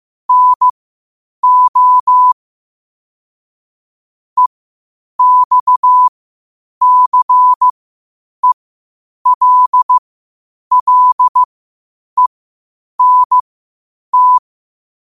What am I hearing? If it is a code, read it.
Morse code, heard as a keyed tone, about 15 wpm, spaced as for 8 wpm, 1000 Hz. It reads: NO EXCELLENT
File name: qotdfiles / 8wpm